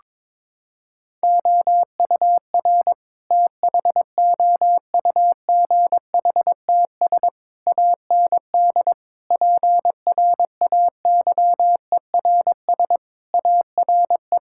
Morse code, heard as a keyed tone, about 22 wpm, 700 Hz. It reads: OUR THOUGHTS AND PRAYERS ARE
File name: practice / 22wpm / Sentences with 7-Word Phrases - 22wpm